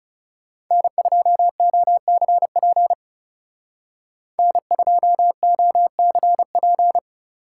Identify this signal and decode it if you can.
Morse code, heard as a keyed tone, about 35 wpm, 700 Hz. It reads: N2OCP N2OCP
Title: Speed Racing - USA Call Signs 30wpm